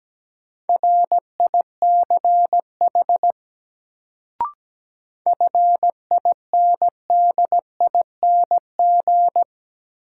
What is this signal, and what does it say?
Morse code, heard as a keyed tone, about 17 wpm, 700 Hz. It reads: RICH FINDING
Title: Sentences from Top 1100 Words - New Word Drill - 17wpm